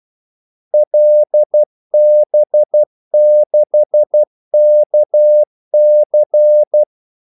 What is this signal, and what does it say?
Morse code, heard as a keyed tone, about 12 wpm, 600 Hz. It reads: LB6KC
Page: 2 ekstra ordmellemrum mellem kaldesignalerne